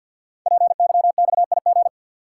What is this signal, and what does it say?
Morse code, heard as a keyed tone, about 51 wpm, 700 Hz. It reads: PYXIG
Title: Code Groups - 5 Letters - 50wpm